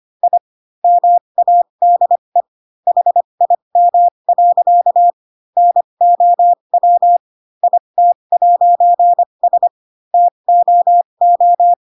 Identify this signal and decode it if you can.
Morse code, heard as a keyed tone, about 25 wpm, spaced as for 22 wpm, 700 Hz. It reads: I MADE HIM. NOW IT'S TOO
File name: Pinocchio / Pinocchio ch3 18 25 ik1hge